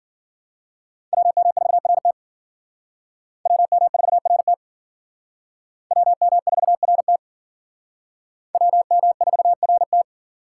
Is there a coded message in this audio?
Yes: Morse code from 1.1 to 10.0 s, heard as a keyed tone, about 52 wpm, 700 Hz. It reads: WM4RT WM4RT WM4RT WM4RT